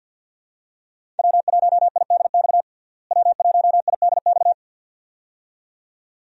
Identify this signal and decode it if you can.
Morse code, heard as a keyed tone, about 50 wpm, 700 Hz. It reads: W1IDX W1IDX
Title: USA Call Signs - 5 Characters - Repeated 2x - 50wpm